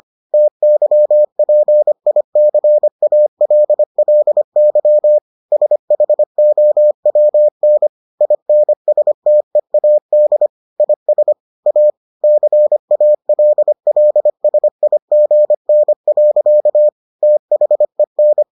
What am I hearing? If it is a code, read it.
Morse code, heard as a keyed tone, about 25 wpm, 600 Hz. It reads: TYPICALLY SHOWN INSTEAD IS A CALLSIGN. THEN